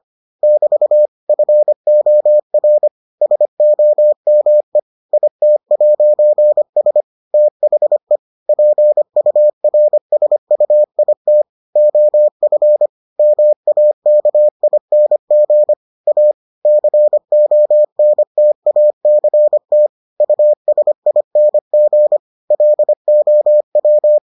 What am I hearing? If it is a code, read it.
Morse code, heard as a keyed tone, about 25 wpm, 600 Hz. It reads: <BT> FOR SOME IT'S THE PURSUIT OF MAKING A CONTACT USING LOW